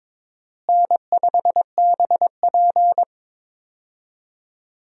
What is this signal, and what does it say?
Morse code, heard as a keyed tone, about 22 wpm, 700 Hz. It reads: N5BP
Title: USA Call Signs - 4 Characters - Repeated 3x - 22wpm